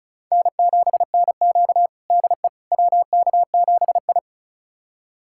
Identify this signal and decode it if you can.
Morse code, heard as a keyed tone, about 34 wpm, 700 Hz. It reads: N7NQ DE WK7I
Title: USA Callsign DE Callsign - 4 Characters - 35wpm